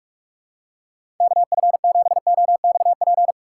Morse code, heard as a keyed tone, about 41 wpm, 700 Hz. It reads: KF7OXP
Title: USA Call Signs - 6 Characters - 4x Character Spacing - 45wpm